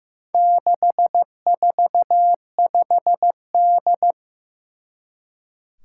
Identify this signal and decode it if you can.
Morse code, heard as a keyed tone, about 15 wpm, 700 Hz. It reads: 645D
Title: Four Challenging Letter-Number Combinations 15wpm